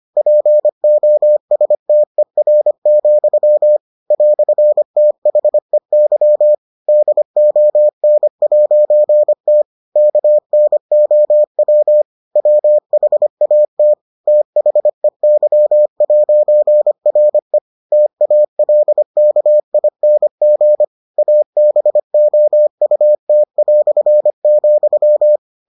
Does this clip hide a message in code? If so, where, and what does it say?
Yes: Morse code from 0.2 to 25.4 s, heard as a keyed tone, about 25 wpm, 600 Hz. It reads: POSTER, "THEY DON'T KNOW WHAT THEY'RE TALKING ABOUT",